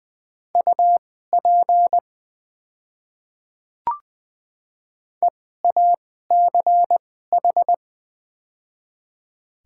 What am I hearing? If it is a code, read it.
Morse code, heard as a keyed tone, about 20 wpm, spaced as for 15 wpm, 700 Hz. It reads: UP EACH
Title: 2x Character Spacing - Top 100 Words - 20wpm